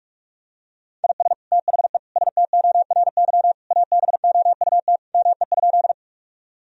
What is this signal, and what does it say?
Morse code, heard as a keyed tone, about 41 wpm, 700 Hz. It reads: IS THE STORY ABOUT ME?
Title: Sentences from Top 1000 Words - Review Part 1 - 45wpm